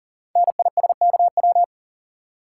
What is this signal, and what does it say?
Morse code, heard as a keyed tone, about 39 wpm, 690 Hz. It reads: NISKW
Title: Code Groups - 5 Letters - 40wpm